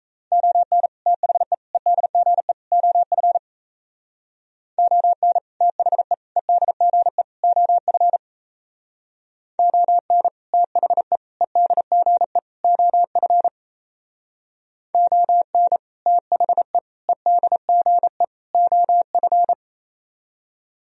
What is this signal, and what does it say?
Morse code, heard as a keyed tone, about 34 wpm, 700 Hz. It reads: ON THE EDGE OF ON THE EDGE OF ON THE EDGE OF ON THE EDGE OF
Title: Speed-Racing - Sets of 4 Words 28wpm